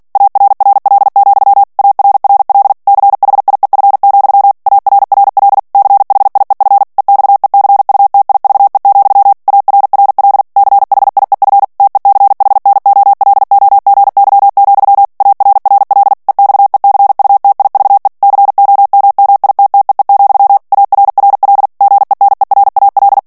Morse code, heard as a keyed tone, about 48 wpm, 800 Hz. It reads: ARRL, ARRL CHIEF, ARRL CHIEF EXECUTIVE, ARRL CHIEF TECHNOLOGY, ARRL EXECUTIVE COMMITTEE, ARRL GENERAL
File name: ANGLAIS texte clair 3 vitesse 45 mots minute